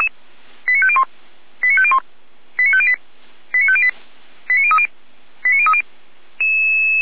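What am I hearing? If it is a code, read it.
Five-tone selective call (ZVEI), heard as quick runs of five tones, tones of about 70 ms. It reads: 89501 89501 89508 89508 89030 89030
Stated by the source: ZVEI five-tone selective call received via SDR, from the Feuerwehr Orlamünde fire brigade audio archive.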